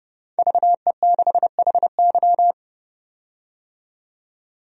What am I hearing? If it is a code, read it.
Morse code, heard as a keyed tone, about 31 wpm, 700 Hz. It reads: VE6HY
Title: DXCC-100 Call Signs - 5 Characters - 30wpm